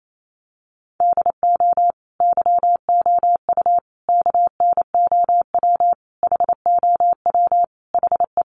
Morse code, heard as a keyed tone, about 28 wpm, 700 Hz. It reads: DO YOU KNOW HOW HE
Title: Sentences from Top 300 Words 28wpm